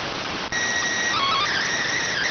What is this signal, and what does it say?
SSTV (slow-scan television), PD 290.
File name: findings / MR AXIAL 33,99mm 1.gen-44 rx